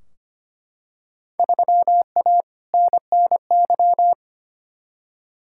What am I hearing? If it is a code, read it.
Morse code, heard as a keyed tone, about 25 wpm, 700 Hz. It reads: 3A NNY
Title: Field Day Exchanges - 25wpm